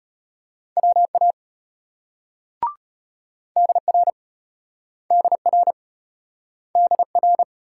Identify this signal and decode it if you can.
Morse code, heard as a keyed tone, about 34 wpm, 700 Hz. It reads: WA DR DR DR